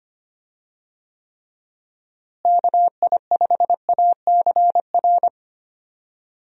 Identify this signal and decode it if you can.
Morse code, heard as a keyed tone, about 25 wpm, 700 Hz. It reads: KI5ACR
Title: USA Call Signs - 6 Characters - Repeated 2x - 25wpm